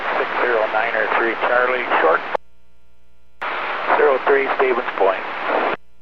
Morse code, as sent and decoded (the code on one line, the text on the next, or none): none